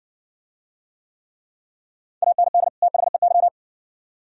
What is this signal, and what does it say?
Morse code, heard as a keyed tone, about 60 wpm, 700 Hz. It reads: AND THEY